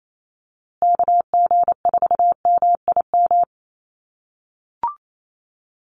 Morse code, heard as a keyed tone, about 28 wpm, 700 Hz. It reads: KG4MIM
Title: USA Call Signs Repeated 3x 28wpm